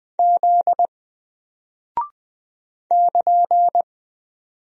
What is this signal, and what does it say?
Morse code, heard as a keyed tone, about 20 wpm, 700 Hz. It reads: Z <KN>
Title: Single Character (Letter:Number:Punctuation:Prosign) Alt 20wpm